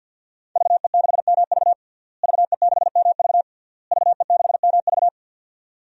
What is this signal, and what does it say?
Morse code, heard as a keyed tone, about 49 wpm, 700 Hz. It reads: VE6MV VE6MV VE6MV
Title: DXCC-100 Call Signs - 5 Characters - Repeated 3x - 50wpm